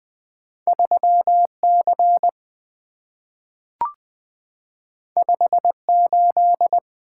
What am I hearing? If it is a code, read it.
Morse code, heard as a keyed tone, about 20 wpm, 700 Hz. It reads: 3C 58
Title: Two Letter-Number Combinations - 20wpm